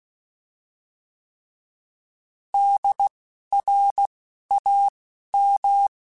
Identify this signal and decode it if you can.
Morse code, heard as a keyed tone, about 16 wpm, spaced as for 12 wpm, 800 Hz. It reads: DRAM
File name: Alphabetical / Alphabetical mp3s / Ds @10